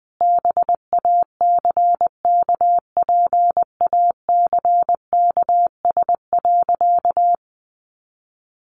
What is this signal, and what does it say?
Morse code, heard as a keyed tone, about 20 wpm, 700 Hz. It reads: BACKPACKS.
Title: Additional Easy Phrases 20wpm